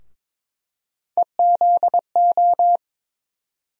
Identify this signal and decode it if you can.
Morse code, heard as a keyed tone, about 22 wpm, 700 Hz. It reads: EZO
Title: USA Call Sign Postfixes - 3 Letters 22wpm